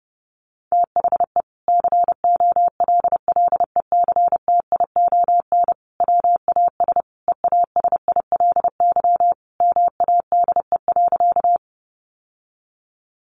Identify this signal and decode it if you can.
Morse code, heard as a keyed tone, about 30 wpm, 700 Hz. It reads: THE COLLECTION WAS EASILY MADE.
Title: Sentences from Top 1200 Words 30wpm